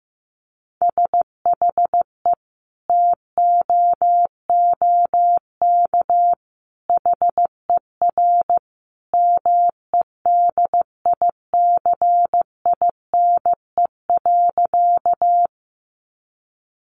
Morse code, heard as a keyed tone, about 15 wpm, 700 Hz. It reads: SHE TOOK HER MEDICINE.
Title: Sentences from Top 900 Words 15wpm